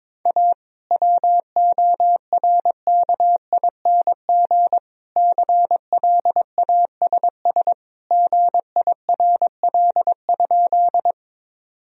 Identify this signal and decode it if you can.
Morse code, heard as a keyed tone, about 22 wpm, 700 Hz. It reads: A WORKING CLASS GIRL?
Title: Sentences from Top 1000 Words - Review Part 4 - 22wpm